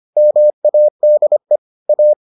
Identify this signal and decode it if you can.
Morse code, heard as a keyed tone, about 25 wpm, 600 Hz. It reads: MADE A